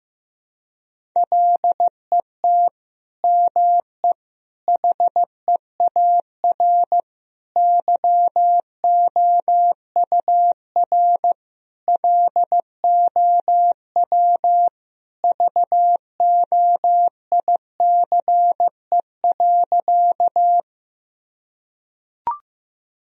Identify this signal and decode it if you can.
Morse code, heard as a keyed tone, about 15 wpm, 700 Hz. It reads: LET ME HEAR YOUR LOW VOICE.
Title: Sentences from Top 1000 Words - Review Part 1 - 15wpm